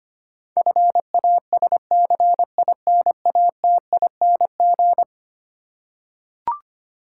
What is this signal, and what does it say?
Morse code, heard as a keyed tone, about 25 wpm, 700 Hz. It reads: FASCINATING
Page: morse-code-ninja-practice-25wpm